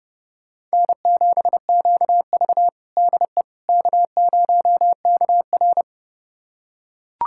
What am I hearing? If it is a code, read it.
Morse code, heard as a keyed tone, about 32 wpm, 700 Hz. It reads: N7QV DE K0KR
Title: USA Callsign DE Callsign - 4 Characters - 30wpm